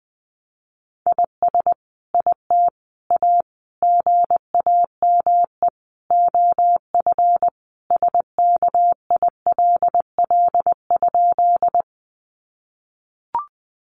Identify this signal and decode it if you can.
Morse code, heard as a keyed tone, about 20 wpm, 700 Hz. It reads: IS IT A GAME OF SKILL?
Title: Sentences from Top 1200 Words 20wpm